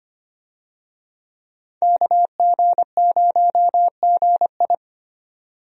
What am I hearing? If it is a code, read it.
Morse code, heard as a keyed tone, about 25 wpm, 700 Hz. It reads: KG0GI
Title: USA Call Signs - 5 Characters - 25wpm